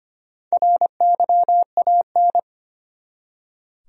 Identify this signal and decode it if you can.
Morse code, heard as a keyed tone, about 25 wpm, 700 Hz. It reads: RYAN
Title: USA Common Names - Max 5 Letters - 25wpm